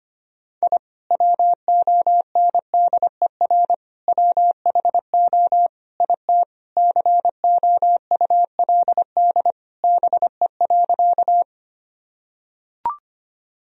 Morse code, heard as a keyed tone, about 25 wpm, 700 Hz. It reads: I WONDER WHO IT COULD BE.
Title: Sentences from Top 1000 Words - Review Part 3 - 25wpm